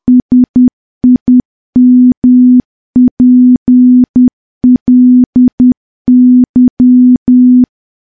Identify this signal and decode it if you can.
Morse code, heard as a keyed tone, about 10 wpm, 250 Hz. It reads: SIMPLY
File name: morse / morse-training-10WPM